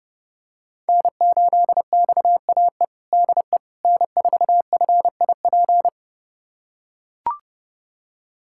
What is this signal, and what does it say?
Morse code, heard as a keyed tone, about 30 wpm, 700 Hz. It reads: N8XAE DE N4FIP
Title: USA Callsign DE Callsign - 5 Characters - 30wpm